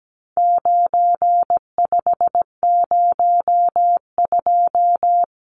Morse code, heard as a keyed tone, about 17 wpm, 700 Hz. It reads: 9502
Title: Numbers - 4-Digit 17wpm